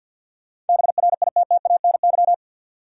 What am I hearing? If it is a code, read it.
Morse code, heard as a keyed tone, about 53 wpm, 700 Hz. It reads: BRITTANY
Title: USA Common Names - 50wpm